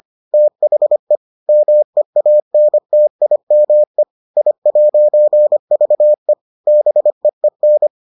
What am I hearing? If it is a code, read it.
Morse code, heard as a keyed tone, about 25 wpm, 600 Hz. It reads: THE MEANTIME I'VE BEEN